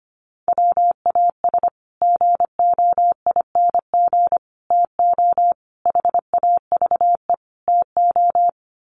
Morse code, heard as a keyed tone, about 25 wpm, 700 Hz. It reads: WAS GOING TO HAVE TO
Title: Sets of 5 Words Encore 25wpm